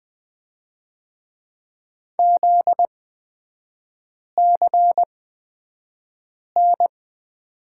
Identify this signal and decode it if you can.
Morse code, heard as a keyed tone, about 20 wpm, spaced as for 5 wpm, 700 Hz. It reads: ZCN
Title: 243 Lesson 39e - Letter z - Three Characters 20wpm